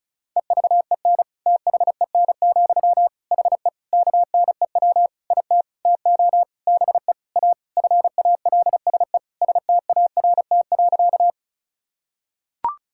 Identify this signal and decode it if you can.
Morse code, heard as a keyed tone, about 36 wpm, 700 Hz. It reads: EVEN THEN, HE KNEW IT TO BE A FALSE START.
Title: Sentences from Top 1600 Words 35wpm